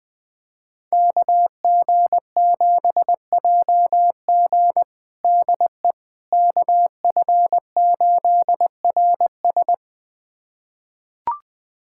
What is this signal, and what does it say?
Morse code, heard as a keyed tone, about 20 wpm, 700 Hz. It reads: KG7JG DE KF8RS
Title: USA Callsign DE Callsign - 5 Characters - 20wpm